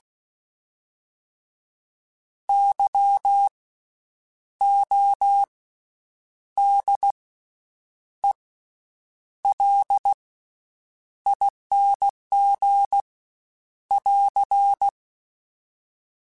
Morse code, heard as a keyed tone, about 16 wpm, spaced as for 8 wpm, 800 Hz. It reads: YODELING <AR>
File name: Special INGs / Special INGs-mp3s / VerySlowToFaster Ing @15